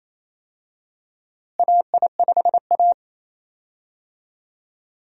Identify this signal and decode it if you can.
Morse code, heard as a keyed tone, about 28 wpm, 700 Hz. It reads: AI5A
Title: USA Call Signs - 4 Characters - Repeated 2x - 28wpm